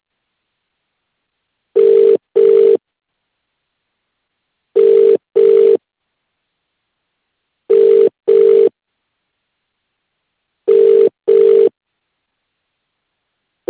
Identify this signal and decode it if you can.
Ringback tone (UK), 400 + 450 Hz.